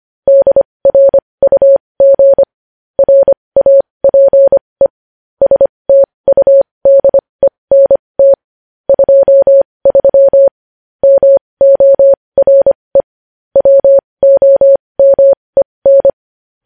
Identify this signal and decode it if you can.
Morse code, heard as a keyed tone, about 25 wpm, spaced as for 20 wpm, 550 Hz. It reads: DRUG RAPE STUDENT 23 MORE WOMEN
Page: News Headlines in Morse Code at 20 WPM